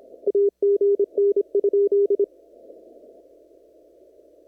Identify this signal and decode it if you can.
Morse code, heard as a keyed tone, about 26 wpm, 400 Hz. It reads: AGN?
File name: CW / kt5j wpx cw-op-k5pi